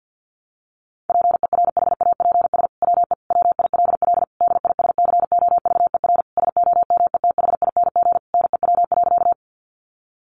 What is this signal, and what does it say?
Morse code, heard as a keyed tone, about 48 wpm, 700 Hz. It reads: PERHAPS WE WILL DISCOVER SOMETHING NEW.
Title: Sentences from Top 1100 Words 50wpm